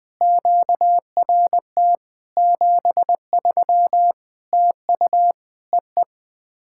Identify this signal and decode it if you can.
Morse code, heard as a keyed tone, about 20 wpm, 700 Hz. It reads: QRT 73 TU EE
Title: POTA QSO Practice - 20wpm